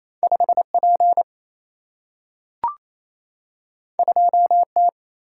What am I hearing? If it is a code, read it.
Morse code, heard as a keyed tone, about 28 wpm, 700 Hz. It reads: 5 P 2 T
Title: Single-Letter Single-Number 28wpm